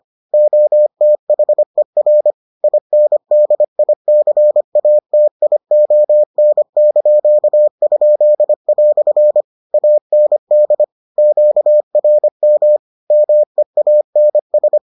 Morse code, heard as a keyed tone, about 25 wpm, 600 Hz. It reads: OTHER INDICATION)?" AND QRM MEANS